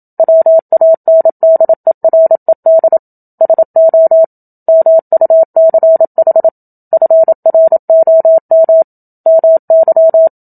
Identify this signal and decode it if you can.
Morse code, heard as a keyed tone, about 27 wpm, 650 Hz. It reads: WANDERED SO MUCH FROM MY
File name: Book Two / 01-Chapter 1 - 27 WPM